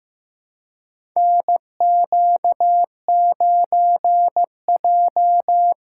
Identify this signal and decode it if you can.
Morse code, heard as a keyed tone, about 15 wpm, 700 Hz. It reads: NQ9J
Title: USA Call Signs - 4 Characters - 3x Character Spacing - 15wpm